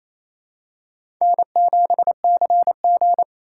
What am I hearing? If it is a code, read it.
Morse code, heard as a keyed tone, about 28 wpm, 700 Hz. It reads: N7CG
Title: USA Call Signs - 4 Characters - 4x Character Spacing - 28wpm